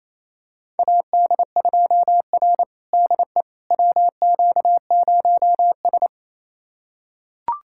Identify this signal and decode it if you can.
Morse code, heard as a keyed tone, about 28 wpm, 690 Hz. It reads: AD2R DE WQ0S
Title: USA Callsign DE Callsign - 4 Characters - 28wpm